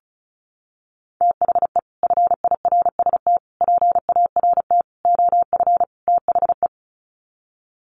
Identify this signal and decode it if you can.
Morse code, heard as a keyed tone, about 34 wpm, 700 Hz. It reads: THE FIRST PART OF THE
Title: Sets of 5 Words Encore 35wpm